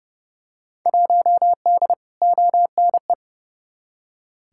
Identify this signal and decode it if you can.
Morse code, heard as a keyed tone, about 30 wpm, 700 Hz. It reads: 1D ONE
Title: Field Day Exchanges - 30wpm